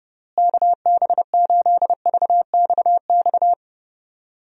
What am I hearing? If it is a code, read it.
Morse code, heard as a keyed tone, about 32 wpm, 700 Hz. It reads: KB8VXX
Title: USA Call Signs - 6 Characters - Repeated 2x - 30wpm